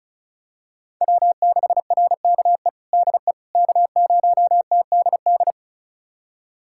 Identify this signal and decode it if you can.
Morse code, heard as a keyed tone, about 34 wpm, 700 Hz. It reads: W6RKE DE K0TDD
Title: USA Callsign DE Callsign - 5 Characters - 35wpm